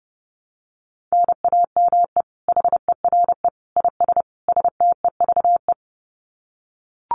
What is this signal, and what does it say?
Morse code, heard as a keyed tone, about 28 wpm, 700 Hz. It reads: NAME HERE IS STEVE
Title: Common QSO Phrases 30wpm